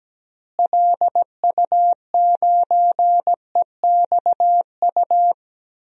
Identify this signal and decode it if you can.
Morse code, heard as a keyed tone, about 17 wpm, 700 Hz. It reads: LU9EXU
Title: DXCC-100 Call Signs - 6 Characters - Repeated 3x - 17wpm